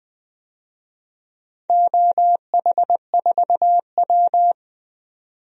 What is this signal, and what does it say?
Morse code, heard as a keyed tone, about 20 wpm, 700 Hz. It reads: OH4W
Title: DXCC-100 Call Signs - 4 Characters - 20wpm